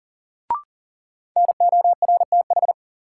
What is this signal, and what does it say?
Morse code, heard as a keyed tone, about 40 wpm, 700 Hz. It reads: NORTH
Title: Sets of 5 Words Encore - New Word Drill - 40wpm